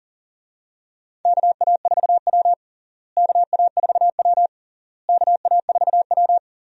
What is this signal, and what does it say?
Morse code, heard as a keyed tone, about 40 wpm, 700 Hz. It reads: KA4W KA4W KA4W
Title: USA Call Signs - 4 Characters - Repeated 3x - 40wpm